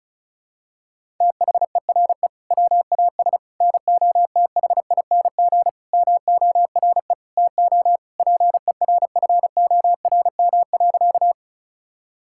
Morse code, heard as a keyed tone, about 34 wpm, 700 Hz. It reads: THERE WAS NOTHING MORE TO PERFORM.
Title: Sentences from Top 1000 Words - Review Part 5 - 35wpm